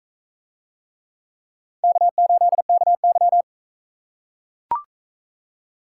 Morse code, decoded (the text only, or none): K8KY